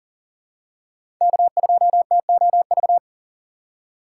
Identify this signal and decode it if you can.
Morse code, heard as a keyed tone, about 40 wpm, 700 Hz. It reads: K2TOV
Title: USA Call Signs - 5 Characters - 2x Character Spacing - 40wpm